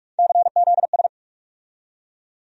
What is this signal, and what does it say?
Morse code, heard as a keyed tone, about 43 wpm, 700 Hz. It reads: CZS
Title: USA Call Sign Postfixes - 3 Letters 45wpm